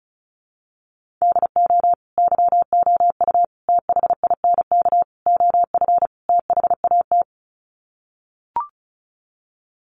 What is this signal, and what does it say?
Morse code, heard as a keyed tone, about 34 wpm, 700 Hz. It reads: DO YOU THINK OF THAT
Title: Sets of 5 Words Encore2 35wpm